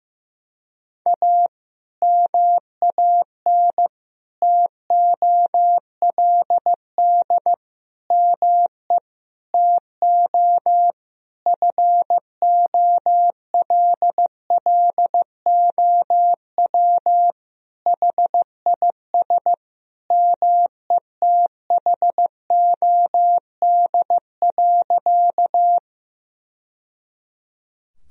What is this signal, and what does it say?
Morse code, heard as a keyed tone, about 15 wpm, 700 Hz. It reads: A MAN TOLD ME TO FOLLOW HIS METHOD.
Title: Sentences from Top 1000 Words - Review Part 3 - 15wpm